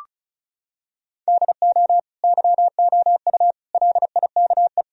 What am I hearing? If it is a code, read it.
Morse code, heard as a keyed tone, about 34 wpm, 700 Hz. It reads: DO YOU LIKE
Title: Sentences from Top 1000 Words - Review Part 4 - 35wpm